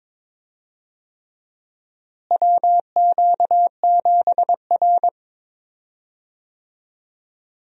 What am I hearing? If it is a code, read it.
Morse code, heard as a keyed tone, about 23 wpm, 700 Hz. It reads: WQ7R
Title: USA Call Signs - 4 Characters - Repeated 3x - 22wpm